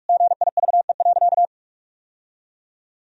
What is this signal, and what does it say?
Morse code, heard as a keyed tone, about 41 wpm, 700 Hz. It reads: GIVE.
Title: Sentences from Top 1000 Words - Review Part 4 - 45wpm